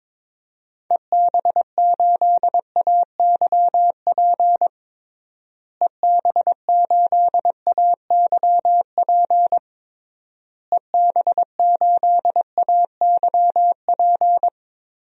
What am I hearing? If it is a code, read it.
Morse code, heard as a keyed tone, about 22 wpm, 700 Hz. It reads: EB8AYP EB8AYP EB8AYP
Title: DXCC-100 Call Signs - 6 Characters - Repeated 3x - 22wpm